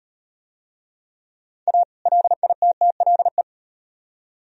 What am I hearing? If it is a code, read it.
Morse code, heard as a keyed tone, about 38 wpm, 700 Hz. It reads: A LITTLE